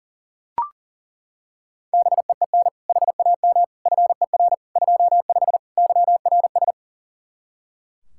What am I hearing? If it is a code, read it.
Morse code, heard as a keyed tone, about 40 wpm, 700 Hz. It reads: BEEN HAM FER 25 YRS
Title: Practice QSO - Exchange 3 - More Info - 40wpm